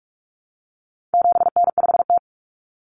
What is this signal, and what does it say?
Morse code, heard as a keyed tone, about 44 wpm, 700 Hz. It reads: 7N5T
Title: Common Cut Numbers - 4-Digit 45wpm